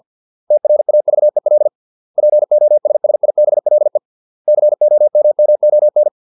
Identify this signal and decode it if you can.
Morse code, heard as a keyed tone, about 50 wpm, 600 Hz. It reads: TRAVEL POSSIBLE COMMON